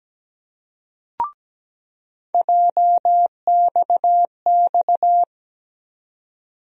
Morse code, heard as a keyed tone, about 17 wpm, 700 Hz. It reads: JXX
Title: USA Call Sign Postfixes - 3 Letters 17wpm